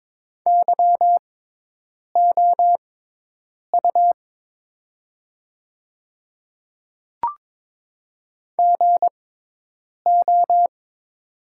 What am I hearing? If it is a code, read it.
Morse code, heard as a keyed tone, about 22 wpm, spaced as for 8 wpm, 700 Hz. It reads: YOU GO
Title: 6x Character Spacing - Top 500 Words - 22wpm